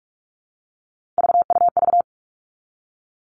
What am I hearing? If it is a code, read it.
Morse code, heard as a keyed tone, about 46 wpm, 700 Hz. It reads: VUV
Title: Three Challenging Letter-Number Combinations 45wpm